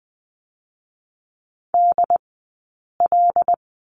Morse code, heard as a keyed tone, about 20 wpm, 700 Hz. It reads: D L
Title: 092 Lesson 16d - Letter c - Two Characters 20wpm